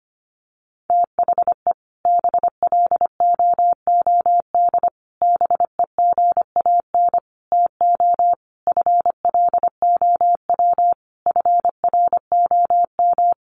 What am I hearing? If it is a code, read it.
Morse code, heard as a keyed tone, about 25 wpm, 700 Hz. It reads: THE BLOOD BEGAN TO FLOW FROM
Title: Sentences from Top 1200 Words 25wpm